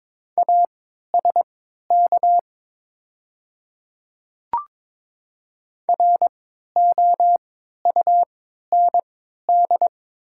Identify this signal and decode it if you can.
Morse code, heard as a keyed tone, about 22 wpm, spaced as for 12 wpm, 700 Hz. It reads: ASK ROUND
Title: 3x Character Spacing - Top 200 Words - 22wpm